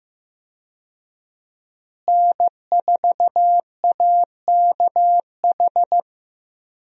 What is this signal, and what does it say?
Morse code, heard as a keyed tone, about 15 wpm, 700 Hz. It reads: N4AKH
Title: USA Call Signs - 5 Characters - Repeated 3x - 15wpm